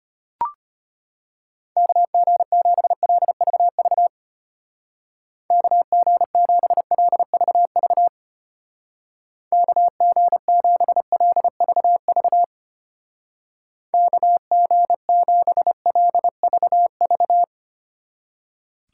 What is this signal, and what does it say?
Morse code, heard as a keyed tone, about 31 wpm, 700 Hz. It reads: KG7LVV KG7LVV KG7LVV KG7LVV
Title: Speed Racing - USA 2x3 Call Signs 25wpm